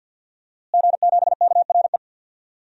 Morse code, heard as a keyed tone, about 48 wpm, 700 Hz. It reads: G7KRE
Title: DXCC-100 Call Signs - 5 Characters - Repeated 3x - 50wpm